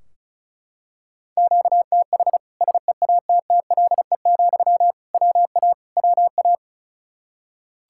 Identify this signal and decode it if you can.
Morse code, heard as a keyed tone, about 36 wpm, 700 Hz. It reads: QTH SEATTLE, WA WA
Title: Practice QSO - Exchange 1 - Beginning - 35wpm